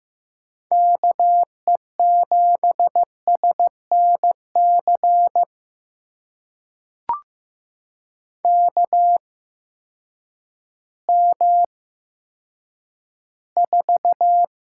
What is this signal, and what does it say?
Morse code, heard as a keyed tone, about 15 wpm, spaced as for 9 wpm, 700 Hz. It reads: KE7SNC KM4
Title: USA Call Signs - 6 Characters - 8x Character Spacing - 15wpm